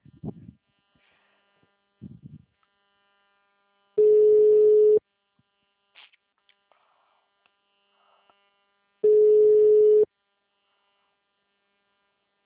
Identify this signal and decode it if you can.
Ringback tone (Europe, 425 Hz).